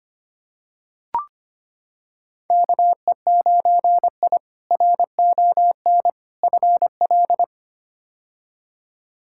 Morse code, heard as a keyed tone, about 25 wpm, 700 Hz. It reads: KE9I RON FL
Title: K1USN SST QSO Practice - 25wpm